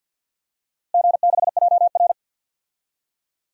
Morse code, heard as a keyed tone, about 50 wpm, 700 Hz. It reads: G6JR
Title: DXCC-100 Call Signs - 4 Characters - Repeated 3x - 50wpm